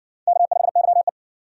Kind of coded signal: Morse code

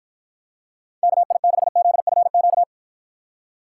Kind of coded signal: Morse code

